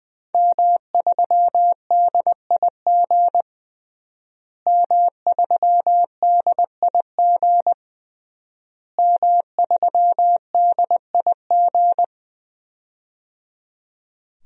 Morse code, heard as a keyed tone, about 20 wpm, 700 Hz. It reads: M3DIG M3DIG M3DIG
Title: DXCC-100 Call Signs - 5 Characters - Repeated 3x - 20wpm